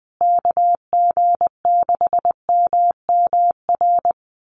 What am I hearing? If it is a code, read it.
Morse code, heard as a keyed tone, about 20 wpm, 700 Hz. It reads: KG6MMR
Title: USA Call Signs Repeated 3x 20wpm